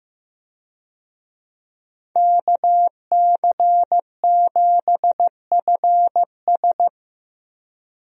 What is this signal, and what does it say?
Morse code, heard as a keyed tone, about 15 wpm, 700 Hz. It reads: KC7FS
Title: USA Call Signs - 5 Characters - Repeated 3x - 15wpm